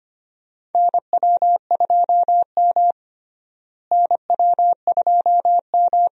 Morse code, heard as a keyed tone, about 25 wpm, 700 Hz. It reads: NW2M NW2M
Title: USA Call Signs - 4 Characters - Repeated 2x - 25wpm